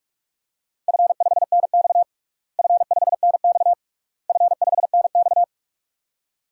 Morse code, heard as a keyed tone, about 46 wpm, 700 Hz. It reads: F5NX F5NX F5NX
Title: DXCC-100 Call Signs - 4 Characters - Repeated 3x - 45wpm